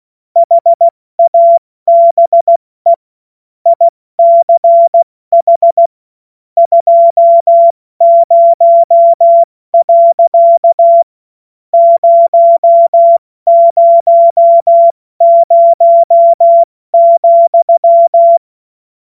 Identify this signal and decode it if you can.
Morse code, heard as a keyed tone, about 16 wpm, spaced as for 13 wpm, 680 Hz. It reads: HABE ICH 20.000,
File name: Morsekurs / Witze / Witz 06. September 2022 70 F80